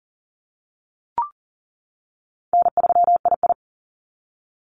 Morse code, heard as a keyed tone, about 40 wpm, 700 Hz. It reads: N3II
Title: NA SOTA Activator Callsigns 40wpm